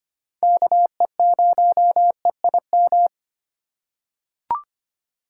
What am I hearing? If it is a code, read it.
Morse code, heard as a keyed tone, about 25 wpm, 700 Hz. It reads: KE0EIM
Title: USA Call Signs - 6 Characters - 2x Character Spacing - 25wpm